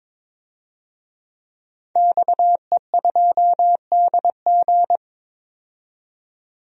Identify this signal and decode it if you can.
Morse code, heard as a keyed tone, about 22 wpm, 700 Hz. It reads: XE2DG
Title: DXCC-100 Call Signs - 5 Characters - 22wpm